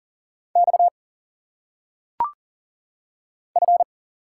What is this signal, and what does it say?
Morse code, heard as a keyed tone, about 40 wpm, 700 Hz. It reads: X F
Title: Single Letters - Phonetic Pronunciation 40wpm